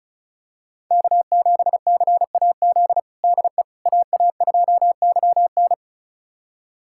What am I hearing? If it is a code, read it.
Morse code, heard as a keyed tone, about 36 wpm, 700 Hz. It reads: K7CAZ DE AA2YN
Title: USA Callsign DE Callsign - 5 Characters - 35wpm